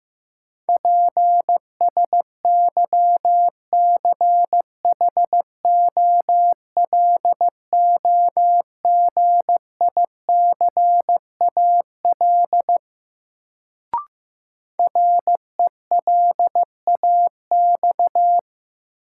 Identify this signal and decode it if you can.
Morse code, heard as a keyed tone, about 15 wpm, 700 Hz. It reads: PSYCHOLOGICAL RELAX
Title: Sentences from Top 1400 Words - New Word Drill Alt - 15wpm